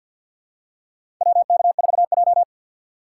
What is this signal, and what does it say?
Morse code, heard as a keyed tone, about 47 wpm, 690 Hz. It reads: WK4J
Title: USA Call Signs - 4 Characters - 6x Character Spacing - 50wpm